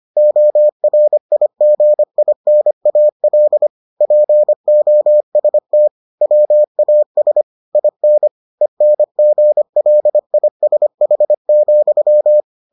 Morse code, heard as a keyed tone, about 25 wpm, 600 Hz. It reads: ORIGINAL POST WAS IN ENGLISH,